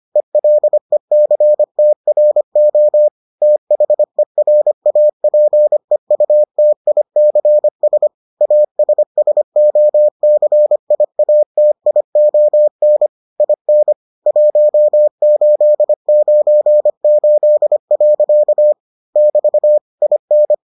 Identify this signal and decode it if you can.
Morse code, heard as a keyed tone, about 25 wpm, 600 Hz. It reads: ELECTRO THERAPEUTICS ASSOCIATION IN 1898. <BT> IN